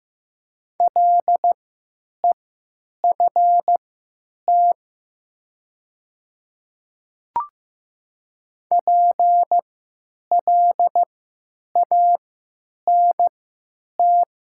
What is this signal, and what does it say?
Morse code, heard as a keyed tone, about 15 wpm, spaced as for 9 wpm, 700 Hz. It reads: LEFT PLANT
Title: 3x Character Spacing - Top 300 Words - 15wpm